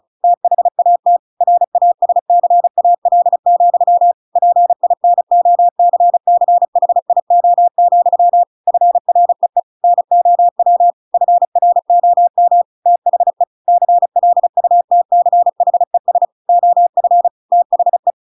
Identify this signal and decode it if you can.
Morse code, heard as a keyed tone, about 34 wpm, 700 Hz. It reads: THAT RASCAL, PINOCCHIO, FREE NOW FROM THE CLUTCHES OF THE